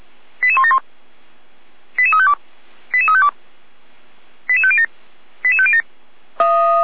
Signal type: five-tone selective call